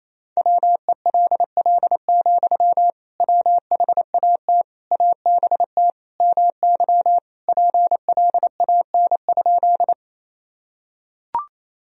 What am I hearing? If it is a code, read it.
Morse code, heard as a keyed tone, about 28 wpm, 700 Hz. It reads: WELL, WHAT ABT MY PLAN?
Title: Sentences from Top 500 Words and Abbreviations 28wpm